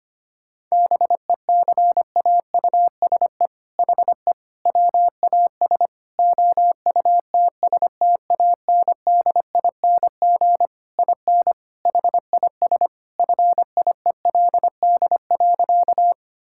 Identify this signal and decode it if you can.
Morse code, heard as a keyed tone, about 25 wpm, 700 Hz. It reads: BECAUSE HE WAS OUTSTANDING IN HIS FIELD.
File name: practice / 25wpm / All-Age Appropriate Jokes - 25wpm